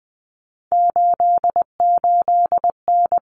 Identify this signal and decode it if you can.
Morse code, heard as a keyed tone, about 20 wpm, 700 Hz. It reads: 88N
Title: Common Cut Numbers - 3-Digit 20wpm